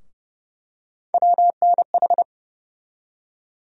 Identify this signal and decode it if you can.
Morse code, heard as a keyed tone, about 30 wpm, 700 Hz. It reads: WNH
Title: USA Call Sign Postfixes - 3 Letters 30wpm